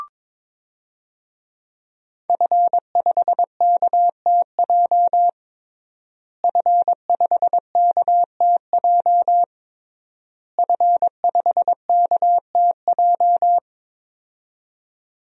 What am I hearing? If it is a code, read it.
Morse code, heard as a keyed tone, about 22 wpm, 700 Hz. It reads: F5KTJ F5KTJ F5KTJ
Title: DXCC-100 Call Signs - 5 Characters - Repeated 3x - 22wpm